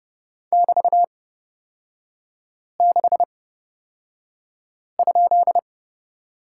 Morse code, heard as a keyed tone, about 30 wpm, spaced as for 5 wpm, 700 Hz. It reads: <BT>6?
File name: Lesson 37e - Letter x - Three Characters 30wpm